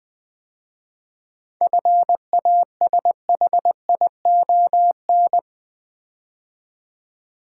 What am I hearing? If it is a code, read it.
Morse code, heard as a keyed tone, about 20 wpm, 700 Hz. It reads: FASHION
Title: 7 Letter Words - First 3 Letters Given - 20wpm